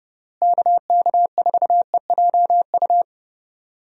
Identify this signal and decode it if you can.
Morse code, heard as a keyed tone, about 32 wpm, 700 Hz. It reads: KK4EJU
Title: USA Call Signs - 6 Characters - Repeated 3x - 30wpm